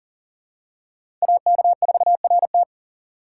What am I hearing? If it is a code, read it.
Morse code, heard as a keyed tone, about 37 wpm, 700 Hz. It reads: AK4RT
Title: USA Call Signs - 5 Characters - 2x Character Spacing - 40wpm